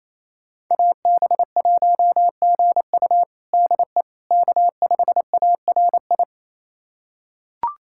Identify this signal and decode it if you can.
Morse code, heard as a keyed tone, about 28 wpm, 700 Hz. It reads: AB1GU DE K5ARI
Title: USA Callsign DE Callsign - 5 Characters - 28wpm